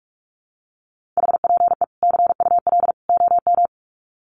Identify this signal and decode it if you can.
Morse code, heard as a keyed tone, about 44 wpm, 700 Hz. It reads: HPE CUL OM
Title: Common QSO Phrases 45wpm